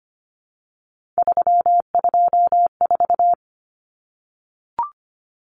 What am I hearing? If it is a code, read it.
Morse code, heard as a keyed tone, about 25 wpm, 700 Hz. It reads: 324
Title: Numbers - 3-Digit 25wpm